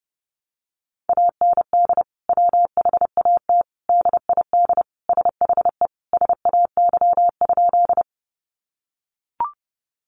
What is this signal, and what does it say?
Morse code, heard as a keyed tone, about 30 wpm, 700 Hz. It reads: AND WHAT DID SHE SAY?
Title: Sentences from Top 200 Words 30wpm